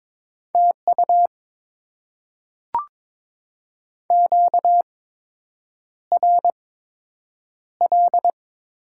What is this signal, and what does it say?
Morse code, heard as a keyed tone, about 22 wpm, spaced as for 8 wpm, 700 Hz. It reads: TU QRL
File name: Top 20 QSO Elements - 8x Character Spacing - 22wpm